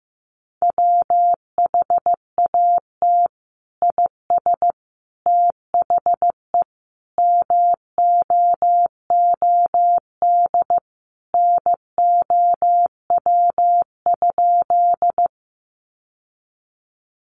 Morse code, heard as a keyed tone, about 15 wpm, 700 Hz. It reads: WHAT IS THE MOOD NOW?
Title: Sentences from Top 1100 Words 15wpm